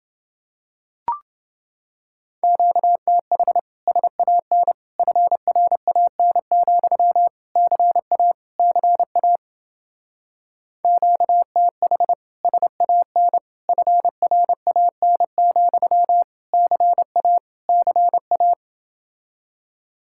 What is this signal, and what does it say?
Morse code, heard as a keyed tone, about 28 wpm, 700 Hz. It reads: QTH SAN FRAN, CA CA QTH SAN FRAN, CA CA